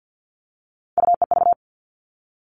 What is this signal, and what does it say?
Morse code, heard as a keyed tone, about 50 wpm, 700 Hz. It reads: UEV
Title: All Cut Numbers - 3-Digit 50wpm